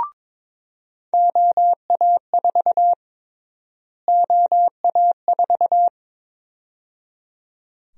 Morse code, heard as a keyed tone, about 21 wpm, 700 Hz. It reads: OA4 OA4
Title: USA Call Sign Prefixes - Repeated 2x - 22wpm